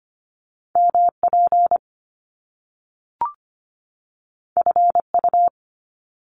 Morse code, heard as a keyed tone, about 25 wpm, 700 Hz. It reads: MP FU
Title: Top Two Letter Combinations in Words 25wpm